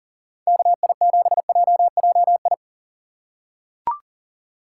Morse code, heard as a keyed tone, about 40 wpm, 700 Hz. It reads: KI7JJI
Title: USA Call Signs - 6 Characters - 6x Character Spacing - 40wpm